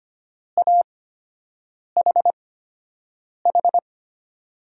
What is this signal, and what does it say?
Morse code, heard as a keyed tone, about 25 wpm, spaced as for 7 wpm, 700 Hz. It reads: AHH
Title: USA Call Signs - 6 Characters - 8x Character Spacing - 25wpm